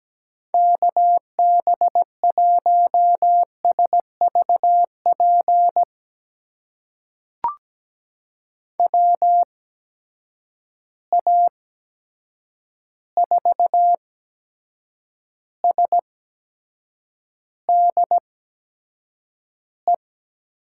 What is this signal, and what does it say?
Morse code, heard as a keyed tone, about 17 wpm, spaced as for 7 wpm, 700 Hz. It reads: KB1SVP WA4SDE
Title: USA Call Signs - 6 Characters - 8x Character Spacing - 17wpm